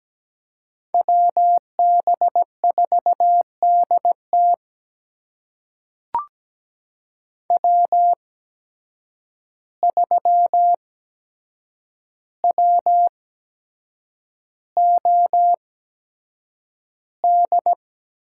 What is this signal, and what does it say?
Morse code, heard as a keyed tone, about 17 wpm, spaced as for 7 wpm, 700 Hz. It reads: WB4DT W3WOD
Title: USA Call Signs - 5 Characters - 8x Character Spacing - 17wpm